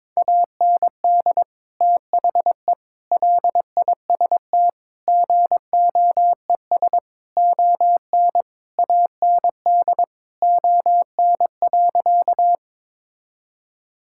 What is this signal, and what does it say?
Morse code, heard as a keyed tone, about 21 wpm, 700 Hz. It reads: AND THE LIST GOES ON AND ON.
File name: Sentences with 7-Word Phrases - 22wpm